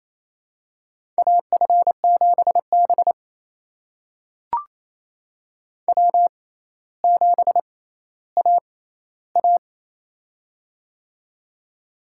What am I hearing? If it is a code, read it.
Morse code, heard as a keyed tone, about 28 wpm, spaced as for 14 wpm, 700 Hz. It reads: AF7B W7AA
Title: USA Call Signs - 4 Characters - 6x Character Spacing - 28wpm